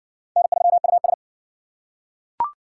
Morse code, heard as a keyed tone, about 50 wpm, 700 Hz. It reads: N3US
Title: Speed Racing - USA 1x2 Call Signs 40wpm